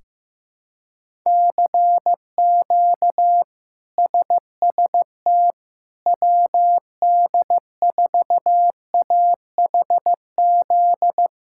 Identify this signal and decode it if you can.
Morse code, heard as a keyed tone, about 15 wpm, 700 Hz. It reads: CQ SST WD4AHZ
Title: K1USN SST QSO Practice - 15wpm